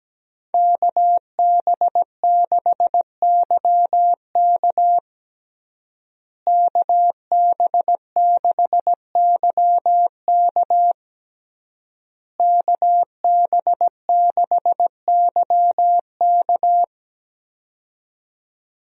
Morse code, heard as a keyed tone, about 17 wpm, 700 Hz. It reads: KB6YK KB6YK KB6YK
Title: USA Call Signs - 5 Characters - Repeated 3x - 17wpm